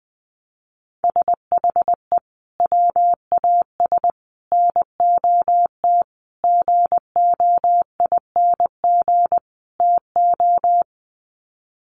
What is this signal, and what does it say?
Morse code, heard as a keyed tone, about 20 wpm, 700 Hz. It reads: SHE WAS NOT GOING TO
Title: Sets of 5 Words Encore2 20wpm